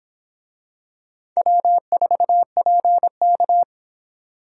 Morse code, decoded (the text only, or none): W4PK